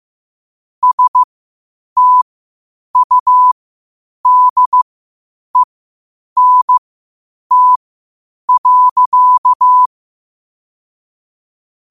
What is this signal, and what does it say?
Morse code, heard as a keyed tone, about 15 wpm, spaced as for 8 wpm, 1000 Hz. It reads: STUDENT.
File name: qotdfiles / 8wpm